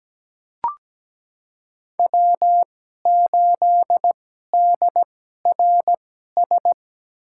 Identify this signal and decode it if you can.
Morse code, heard as a keyed tone, about 17 wpm, spaced as for 12 wpm, 700 Hz. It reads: W8DRS
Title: USA Call Signs - 5 Characters - 2x Character Spacing - 17wpm